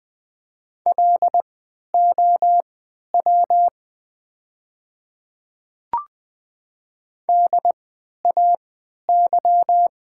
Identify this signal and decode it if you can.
Morse code, heard as a keyed tone, about 20 wpm, spaced as for 11 wpm, 700 Hz. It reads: LOW DAY
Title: 3x Character Spacing - Top 200 Words - 20wpm